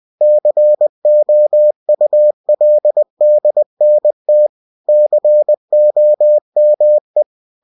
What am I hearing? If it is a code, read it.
Morse code, heard as a keyed tone, about 20 wpm, 600 Hz. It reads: COULDNT COME
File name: cw-books / winnie-the-pooh / winnie-20wpm-page-0095